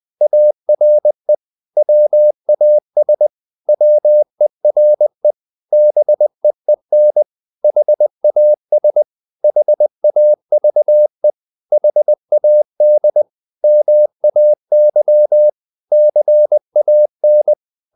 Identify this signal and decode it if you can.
Morse code, heard as a keyed tone, about 20 wpm, 600 Hz. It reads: ARE WAS WERE BEEN HAS HAVE HAD MAY CAN
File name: CW / Practice / 500MostCommonEnglishWords20